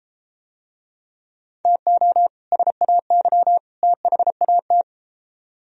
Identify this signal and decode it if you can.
Morse code, heard as a keyed tone, about 32 wpm, 700 Hz. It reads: TO SAY THAT